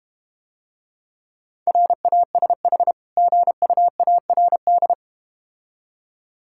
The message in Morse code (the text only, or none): RASH GUARD